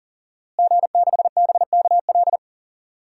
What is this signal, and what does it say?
Morse code, heard as a keyed tone, about 40 wpm, 700 Hz. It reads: G6BKL
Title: DXCC-100 Call Signs - 5 Characters - Repeated 3x - 40wpm